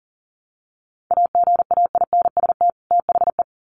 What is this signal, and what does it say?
Morse code, heard as a keyed tone, about 40 wpm, 700 Hz. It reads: AGAINST THE
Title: Sets of 2 Words 40wpm